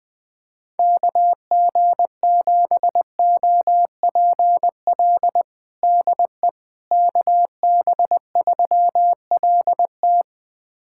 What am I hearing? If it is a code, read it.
Morse code, heard as a keyed tone, about 20 wpm, 700 Hz. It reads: KG7OPL DE KB3LT
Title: USA Callsign DE Callsign 20wpm